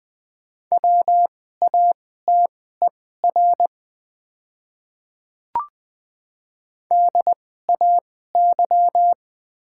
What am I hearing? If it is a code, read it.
Morse code, heard as a keyed tone, about 20 wpm, spaced as for 14 wpm, 700 Hz. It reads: WATER DAY
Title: 2x Character Spacing - Top 200 Words - 20wpm